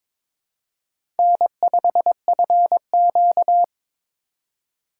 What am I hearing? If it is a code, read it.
Morse code, heard as a keyed tone, about 22 wpm, 700 Hz. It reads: N5FQ
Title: USA Call Signs - 4 Characters - 3x Character Spacing - 22wpm